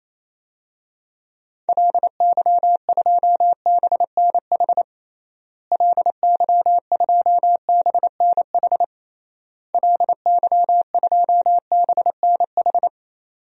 Morse code, heard as a keyed tone, about 28 wpm, 700 Hz. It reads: LY2BNH LY2BNH LY2BNH
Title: DXCC-100 Call Signs - 6 Characters - Repeated 3x - 28wpm